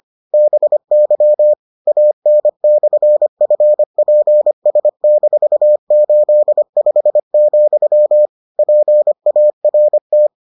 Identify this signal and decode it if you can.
Morse code, heard as a keyed tone, about 25 wpm, 600 Hz. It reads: BY AN/FPS-85, PART